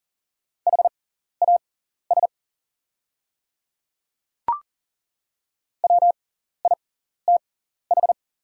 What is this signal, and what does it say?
Morse code, heard as a keyed tone, about 40 wpm, spaced as for 14 wpm, 690 Hz. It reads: HAS WITH
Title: 6x Character Spacing - Top 100 Words - 40wpm